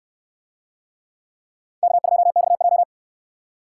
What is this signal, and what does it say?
Morse code, heard as a keyed tone, about 69 wpm, 700 Hz. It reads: K2XJ